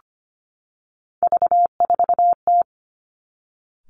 Morse code, heard as a keyed tone, about 25 wpm, 700 Hz. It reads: V4T
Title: Three Challenging Letter-Number Combinations 25wpm